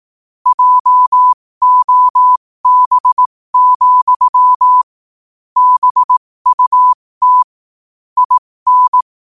Morse code, heard as a keyed tone, about 18 wpm, spaced as for 15 wpm, 1000 Hz. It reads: JOB, BUT IN
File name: qotdfiles / 15wpm